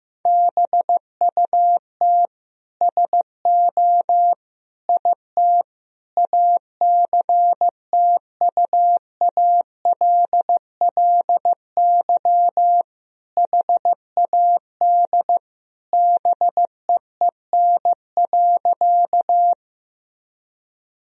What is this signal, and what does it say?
Morse code, heard as a keyed tone, about 16 wpm, 700 Hz. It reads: BUT SO IT ACTUALLY HAD BEEN.
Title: Sentences from Top 1000 Words - Review Part 4 - 15wpm